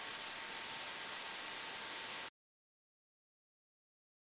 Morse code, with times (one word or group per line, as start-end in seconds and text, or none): none